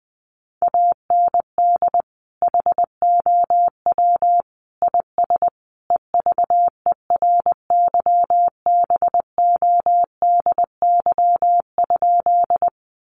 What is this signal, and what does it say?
Morse code, heard as a keyed tone, about 20 wpm, 700 Hz. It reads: AND HOW IS EVERYBODY?
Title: Sentences from Top 1400 Words 20wpm